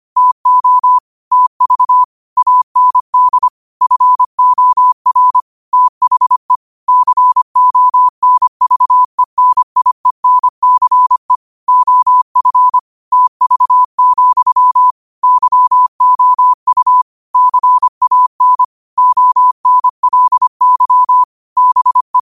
Morse code, heard as a keyed tone, about 25 wpm, 1000 Hz. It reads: TO TV AND FOR THE CONVENIENCE OF TV, YOU CAN ONLY BE